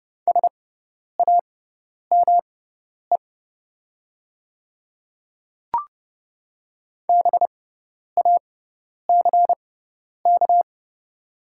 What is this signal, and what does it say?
Morse code, heard as a keyed tone, about 30 wpm, spaced as for 10 wpm, 700 Hz. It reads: SAME BACK
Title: 6x Character Spacing - Top 300 Words - 30wpm